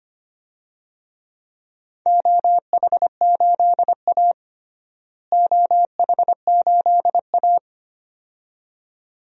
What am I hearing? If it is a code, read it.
Morse code, heard as a keyed tone, about 25 wpm, 700 Hz. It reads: OH8A OH8A
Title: DXCC-100 Call Signs - 4 Characters - Repeated 2x - 25wpm